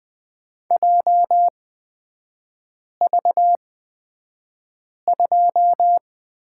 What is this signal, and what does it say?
Morse code, heard as a keyed tone, about 20 wpm, spaced as for 5 wpm, 690 Hz. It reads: JV2
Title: 229 Lesson 37e - Letter x - Three Characters 20wpm